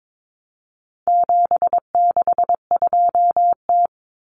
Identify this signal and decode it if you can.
Morse code, heard as a keyed tone, about 22 wpm, 700 Hz. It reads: 762T
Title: Common Cut Numbers - 4-Digit 22wpm